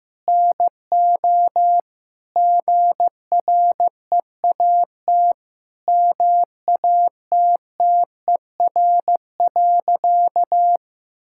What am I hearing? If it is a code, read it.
Morse code, heard as a keyed tone, about 15 wpm, 700 Hz. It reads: NO GREAT MATTER.
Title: Sentences from Top 1600 Words 15wpm